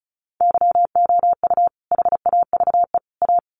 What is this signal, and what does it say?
Morse code, heard as a keyed tone, about 34 wpm, 700 Hz. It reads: YOU HAVE A
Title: Sets of 3 Words 35wpm